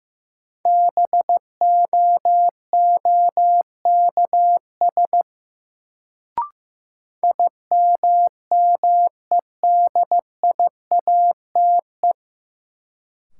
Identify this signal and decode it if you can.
Morse code, heard as a keyed tone, about 15 wpm, 700 Hz. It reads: BOOKS IMMEDIATE
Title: Sentences from Top 1200 Words - New Word Drill Alt - 15wpm